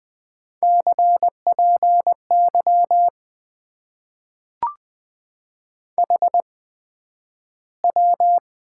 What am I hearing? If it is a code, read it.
Morse code, heard as a keyed tone, about 20 wpm, spaced as for 11 wpm, 700 Hz. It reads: CPY HW
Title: Top 90 QSO Elements - 8x Character Spacing - 20wpm